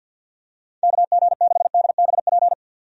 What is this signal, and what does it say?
Morse code, heard as a keyed tone, about 47 wpm, 700 Hz. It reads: KG6DBP
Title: USA Call Signs - 6 Characters - 50wpm